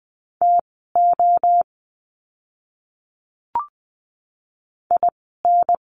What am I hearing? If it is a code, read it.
Morse code, heard as a keyed tone, about 20 wpm, spaced as for 15 wpm, 700 Hz. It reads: TO IN
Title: 192 Lesson 31f - Letter v - Words 20wpm